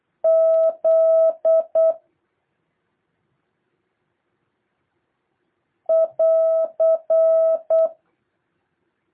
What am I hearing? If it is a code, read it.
Morse code, heard as a keyed tone, about 8 wpm, 650 Hz. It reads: Z <AR>
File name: jh8cbh / 2023cw